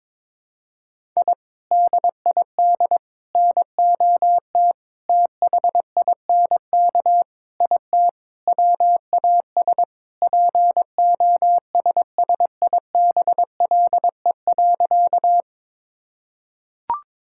Morse code, heard as a keyed tone, about 22 wpm, 700 Hz. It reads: I DID NOT THINK IT WAS POSSIBLE.
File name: Sentences with 4-Word Phrases - 22wpm